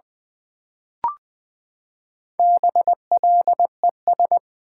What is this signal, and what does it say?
Morse code, heard as a keyed tone, about 20 wpm, 700 Hz. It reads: BLES
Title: Build A Word - 4 Letters At A Time - 8-Letter Words - 20wpm